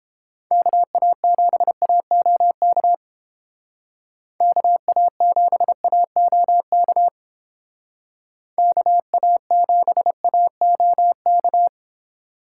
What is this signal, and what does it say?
Morse code, heard as a keyed tone, about 29 wpm, 700 Hz. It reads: KA7AOK KA7AOK KA7AOK